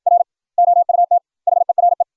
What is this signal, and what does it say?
Morse code, heard as a keyed tone, about 61 wpm, 690 Hz. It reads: R OUT HERE